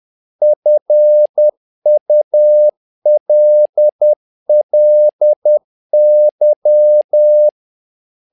Morse code, heard as a keyed tone, about 10 wpm, 600 Hz. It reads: FULLY
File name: learning center / tower trouble / tower trouble 10wpm